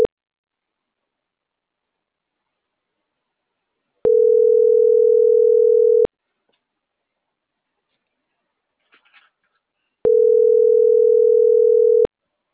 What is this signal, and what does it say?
Ringback tone (North America), 440 + 480 Hz.